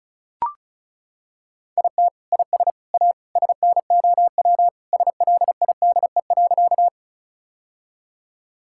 Morse code, heard as a keyed tone, about 34 wpm, 700 Hz. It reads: IT IS A SNOW SLIDE.
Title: Sentences from Top 1600 Words 35wpm